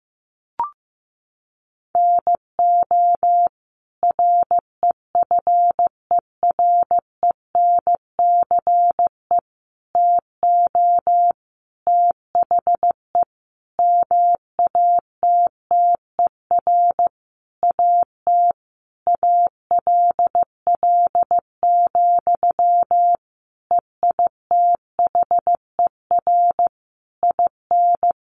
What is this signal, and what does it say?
Morse code, heard as a keyed tone, about 15 wpm, 700 Hz. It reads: NO REFERENCE TO THE MATTER AT ALL, EITHER IN
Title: Sentences from Top 1200 Words 15wpm